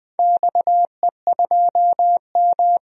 Morse code, heard as a keyed tone, about 20 wpm, 700 Hz. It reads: XE2M
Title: DXCC-100 Call Signs - 4 Characters - 20wpm